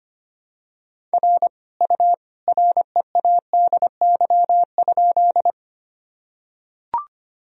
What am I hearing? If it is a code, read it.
Morse code, heard as a keyed tone, about 25 wpm, 700 Hz. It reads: R U READY?
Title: Sentences from Top 400 Words and Abbreviations 25wpm